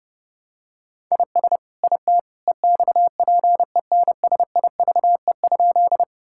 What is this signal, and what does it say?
Morse code, heard as a keyed tone, about 30 wpm, 700 Hz. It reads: IS IT EXPENSIVE?
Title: Sentences from Top 1000 Words - Review Part 6 - 30wpm